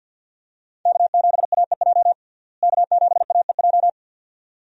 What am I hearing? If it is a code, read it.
Morse code, heard as a keyed tone, about 50 wpm, 700 Hz. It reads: K7AEJ K7AEJ
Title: USA Call Signs - 5 Characters - Repeated 2x - 50wpm